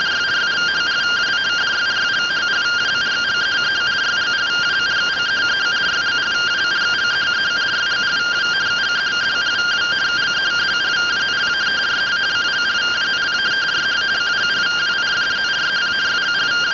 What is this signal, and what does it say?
Two-tone teleprinter signal, about 170 Hz shift.